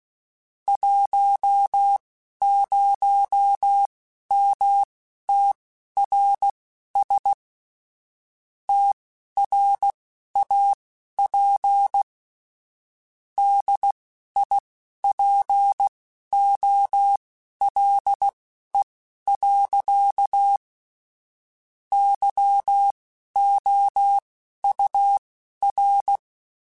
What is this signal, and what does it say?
Morse code, heard as a keyed tone, about 16 wpm, spaced as for 11 wpm, 800 Hz. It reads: 10MTRS TRAP DIPOLE. YOUR
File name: QSOs / QSOs-mp3s / WD0FBT 1@15